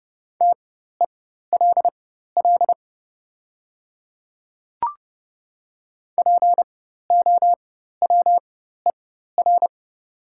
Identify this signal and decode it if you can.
Morse code, heard as a keyed tone, about 31 wpm, spaced as for 14 wpm, 700 Hz. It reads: TELL POWER
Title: 4x Character Spacing - Top 500 Words - 30wpm